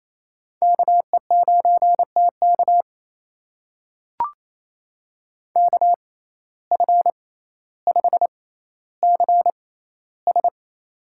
Morse code, heard as a keyed tone, about 27 wpm, spaced as for 14 wpm, 700 Hz. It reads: KE9TK KF5CS
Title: USA Call Signs - 5 Characters - 6x Character Spacing - 28wpm